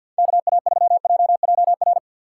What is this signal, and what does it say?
Morse code, heard as a keyed tone, about 47 wpm, 700 Hz. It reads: KA3JJR
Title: USA Call Signs - 4x Character Spacing - 50wpm